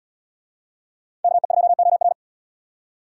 Morse code, heard as a keyed tone, about 77 wpm, 700 Hz. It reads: KE2OR